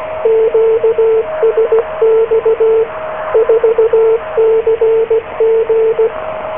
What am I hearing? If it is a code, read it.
Morse code, heard as a keyed tone, about 16 wpm, 450 Hz, with radio noise. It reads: QSX 4CG